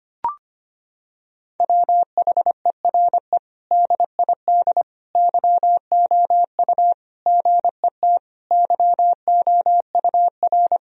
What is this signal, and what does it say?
Morse code, heard as a keyed tone, about 25 wpm, 700 Hz. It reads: WHERE DID YOU GET YOUR
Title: Sentences from Top 1000 Words - Review Part 4 - 25wpm